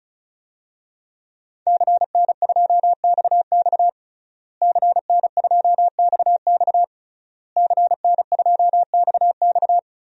Morse code, heard as a keyed tone, about 34 wpm, 700 Hz. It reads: CN2XX CN2XX CN2XX
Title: DXCC-100 Call Signs - 5 Characters - Repeated 3x - 35wpm